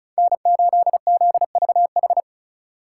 Morse code, heard as a keyed tone, about 46 wpm, 700 Hz. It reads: N8ZVH